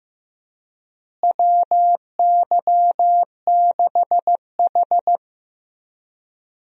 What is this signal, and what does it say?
Morse code, heard as a keyed tone, about 15 wpm, 700 Hz. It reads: WY6H
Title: USA Call Signs - 4 Characters - Repeated 3x - 15wpm